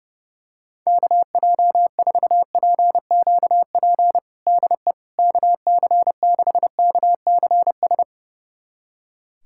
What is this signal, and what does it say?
Morse code, heard as a keyed tone, about 31 wpm, 700 Hz. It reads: KJ4PQP DE KC6KCS
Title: USA Callsign DE Callsign - 6 Characters - 30wpm